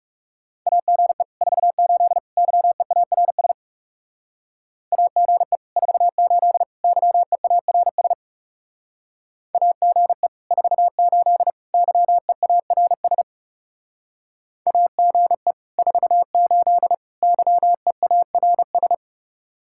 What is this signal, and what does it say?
Morse code, heard as a keyed tone, about 36 wpm, 700 Hz. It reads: AGE 48 YEARS AGE 48 YEARS AGE 48 YEARS AGE 48 YEARS
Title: Speed-Racing Common QSO Phrases Encore - 30wpm